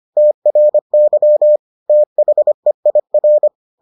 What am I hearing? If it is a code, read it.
Morse code, heard as a keyed tone, about 25 wpm, 600 Hz. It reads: TRY THEIR